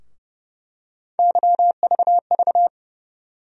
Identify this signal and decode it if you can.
Morse code, heard as a keyed tone, about 32 wpm, 700 Hz. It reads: YVV
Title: USA Call Sign Postfixes - 3 Letters 30wpm